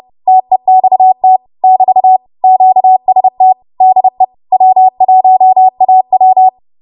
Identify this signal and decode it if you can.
Morse code, heard as a keyed tone, about 30 wpm, 750 Hz. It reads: TEXT <BT> QST DE W1AW